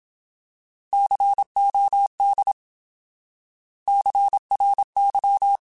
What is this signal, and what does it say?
Morse code, heard as a keyed tone, about 27 wpm, 800 Hz. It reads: COD CRY
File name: Alphabetical / Alphabetical mp3s / Cs @25